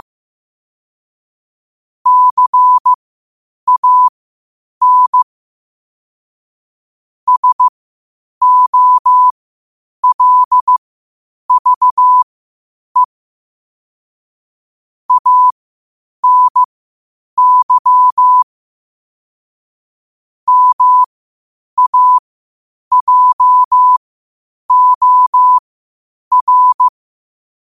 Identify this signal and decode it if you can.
Morse code, heard as a keyed tone, about 15 wpm, spaced as for 8 wpm, 1000 Hz. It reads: CAN SOLVE ANY MAJOR
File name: qotdfiles / 8wpm